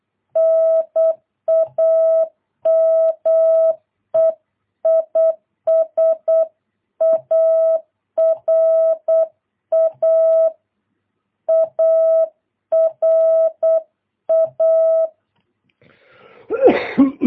Morse code, heard as a keyed tone, about 8 wpm, 650 Hz. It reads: NAME IS ARA ARA
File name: jh8cbh / 2022cw